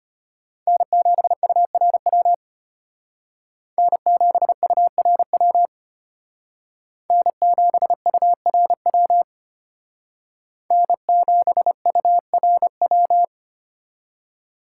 Morse code, heard as a keyed tone, about 30 wpm, 700 Hz. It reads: N7URW N7URW N7URW N7URW
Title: Speed Racing - USA 1x3 Call Signs 25wpm